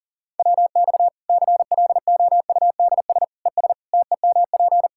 Morse code, heard as a keyed tone, about 40 wpm, 700 Hz. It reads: WX CLOUDS ES TEMP
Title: Practice QSO - Exchange 2 - Info - 40wpm